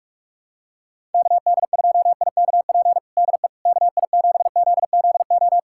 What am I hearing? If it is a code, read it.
Morse code, heard as a keyed tone, about 46 wpm, 700 Hz. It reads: KD2IKP DE KI7ZZO
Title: USA Callsign DE Callsign - 6 Characters - 45wpm